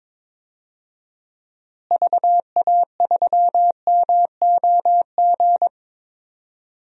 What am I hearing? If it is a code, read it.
Morse code, heard as a keyed tone, about 22 wpm, 700 Hz. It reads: VA3MOG
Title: DXCC-100 Call Signs - 6 Characters - Repeated 2x - 22wpm